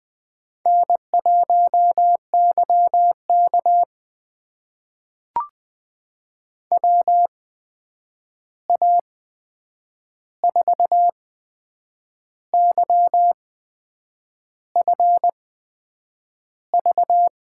Morse code, heard as a keyed tone, about 20 wpm, spaced as for 7 wpm, 700 Hz. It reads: N1YK WA4YFV
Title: USA Call Signs - 8x Character Spacing - 20wpm